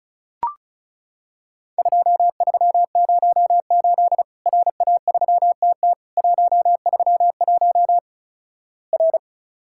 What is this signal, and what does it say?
Morse code, heard as a keyed tone, about 36 wpm, 700 Hz. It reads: 2308 RA3TT 131 R
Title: Worked All Europe DX Contest - QTC Exchange - 35wpm